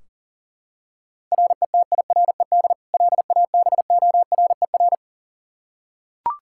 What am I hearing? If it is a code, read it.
Morse code, heard as a keyed tone, about 37 wpm, 700 Hz. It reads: RETIRED LABORER
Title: K1USN SST QSO Practice 40wpm